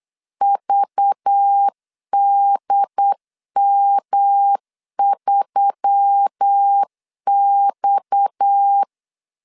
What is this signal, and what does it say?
Morse code, heard as a keyed tone, about 8 wpm, 800 Hz. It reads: VDM3X
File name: LEZIONI / 8 YJ45 / 8 YJ 8wpm 3313